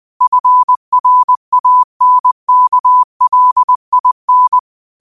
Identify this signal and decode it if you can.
Morse code, heard as a keyed tone, about 20 wpm, 1000 Hz. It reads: FRANKLIN
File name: qotdfiles / 20wpm